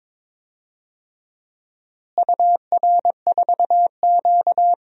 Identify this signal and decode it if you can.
Morse code, heard as a keyed tone, about 22 wpm, 700 Hz. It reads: UR4Q
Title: DXCC-100 Call Signs - 4 Characters - Repeated 2x - 22wpm